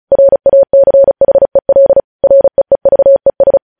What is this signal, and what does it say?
Morse code, heard as a keyed tone, about 36 wpm, 550 Hz. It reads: RACHEL REEVES
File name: podcast / News headlines-1x-35x35